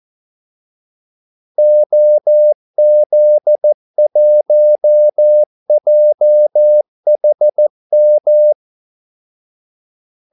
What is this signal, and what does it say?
Morse code, heard as a keyed tone, about 14 wpm, 600 Hz. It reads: OZ1JHM
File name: Callsigns-14wpm